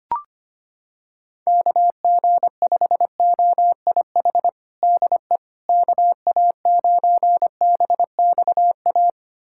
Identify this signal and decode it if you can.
Morse code, heard as a keyed tone, about 25 wpm, 700 Hz. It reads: KG5OIH DE KA9BXA
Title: USA Callsign DE Callsign - 6 Characters - 25wpm